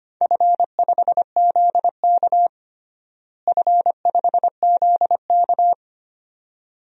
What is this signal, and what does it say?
Morse code, heard as a keyed tone, about 25 wpm, 700 Hz. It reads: F5ZK F5ZK
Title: DXCC-100 Call Signs - 4 Characters - Repeated 2x - 25wpm